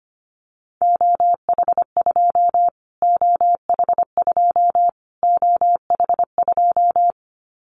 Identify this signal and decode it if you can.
Morse code, heard as a keyed tone, about 25 wpm, 700 Hz. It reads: OH2 OH2 OH2
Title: Callsign Prefixes for Top 25 DXCC - Repeated 3x - 25wpm